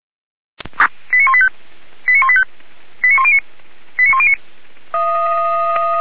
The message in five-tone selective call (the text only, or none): 89185 89185 89109 89109